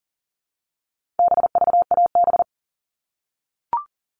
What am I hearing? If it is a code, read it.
Morse code, heard as a keyed tone, about 40 wpm, 700 Hz. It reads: BVAB
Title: All Cut Numbers - 4-Digit 40wpm